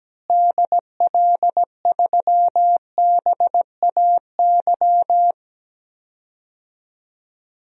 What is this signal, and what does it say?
Morse code, heard as a keyed tone, about 17 wpm, 700 Hz. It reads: DL3BAY
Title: DXCC-100 Call Signs - 6 Characters - Repeated 3x - 17wpm